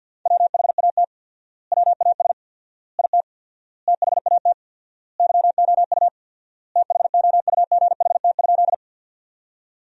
Morse code, heard as a keyed tone, about 52 wpm, 700 Hz. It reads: WHAT WAS IT THAT YOU THOUGHT?
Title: 4x Word-Spacing - Sentences from Top 300 Words - 50wpm